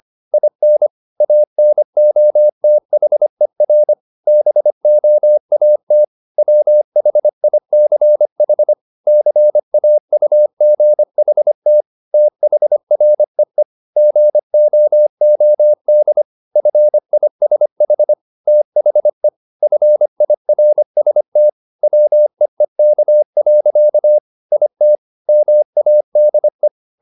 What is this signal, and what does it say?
Morse code, heard as a keyed tone, about 25 wpm, 600 Hz. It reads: IN ANOTHER BOAT WHICH CAUGHT THREE GOOD FISH THE FIRST WEEK. IT MADE
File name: the old man and the sea part 1 25wpm